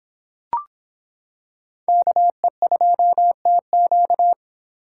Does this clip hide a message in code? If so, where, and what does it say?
Yes: Morse code from 1.9 to 4.3 s, heard as a keyed tone, about 24 wpm, 700 Hz. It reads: KE2TQ